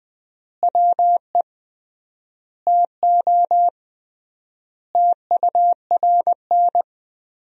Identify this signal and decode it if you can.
Morse code, heard as a keyed tone, about 20 wpm, 700 Hz. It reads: WE TO TURN
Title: 3x Word-Spacing - Sentences from Top 200 Words - 20wpm